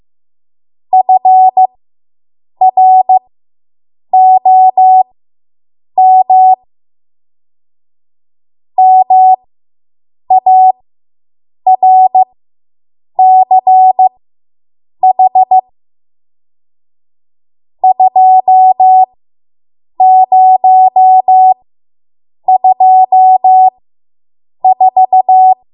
Morse code, heard as a keyed tone, about 15 wpm, spaced as for 7 wpm, 750 Hz. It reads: FROM MARCH 2024